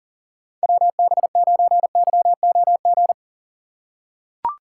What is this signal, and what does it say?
Morse code, heard as a keyed tone, about 38 wpm, 700 Hz. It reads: WB9YOG
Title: USA Call Signs - 6 Characters - 3x Character Spacing - 40wpm